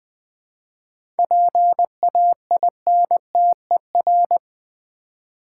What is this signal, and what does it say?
Morse code, heard as a keyed tone, about 20 wpm, 700 Hz. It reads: PAINTER
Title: 7 Letter Words - First 3 Letters Given - 20wpm